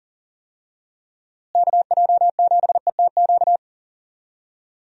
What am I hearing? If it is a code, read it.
Morse code, heard as a keyed tone, about 40 wpm, 700 Hz. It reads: KJ7ETQ
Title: USA Call Signs - 6 Characters - Repeated 2x - 40wpm